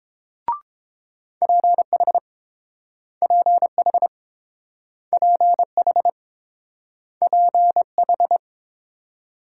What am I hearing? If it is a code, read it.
Morse code, heard as a keyed tone, about 29 wpm, 700 Hz. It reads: PH PH PH PH